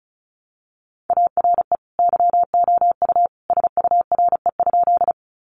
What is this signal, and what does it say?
Morse code, heard as a keyed tone, about 34 wpm, 700 Hz. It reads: ARE YOU SURE?
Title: Sentences from Top 300 Words 35wpm